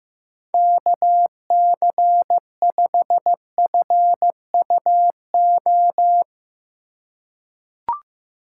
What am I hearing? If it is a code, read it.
Morse code, heard as a keyed tone, about 15 wpm, 700 Hz. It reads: KC5FUO
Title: USA Call Signs - 4x Character Spacing - 15wpm